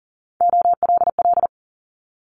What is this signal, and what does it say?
Morse code, heard as a keyed tone, about 42 wpm, 700 Hz. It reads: OLL
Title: Multi-letter Phonemes Quick Fire 40wpm